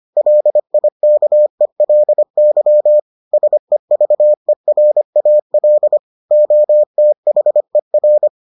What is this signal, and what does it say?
Morse code, heard as a keyed tone, about 25 wpm, 600 Hz. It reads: LIKELY SEVERAL OTHER